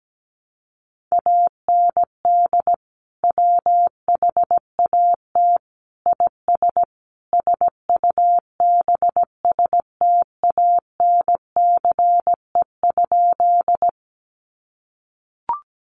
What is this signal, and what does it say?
Morse code, heard as a keyed tone, about 17 wpm, 700 Hz. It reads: AND WHAT IS SUBSTANCE?
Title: Sentences from Top 1500 Words 17wpm